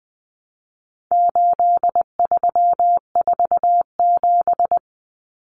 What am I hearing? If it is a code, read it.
Morse code, heard as a keyed tone, about 20 wpm, 700 Hz. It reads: 8347
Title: Numbers - 4-Digit 20wpm